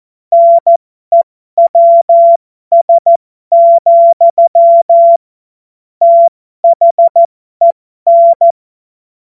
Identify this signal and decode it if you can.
Morse code, heard as a keyed tone, about 14 wpm, spaced as for 12 wpm, 680 Hz. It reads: NEWS, THEN
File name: Morsekurs / Witze